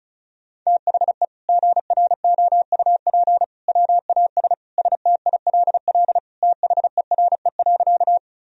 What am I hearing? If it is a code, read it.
Morse code, heard as a keyed tone, about 36 wpm, 700 Hz. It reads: THE GROUP WAS STILL THERE.
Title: Sentences from Top 1000 Words - Review Part 1 - 35wpm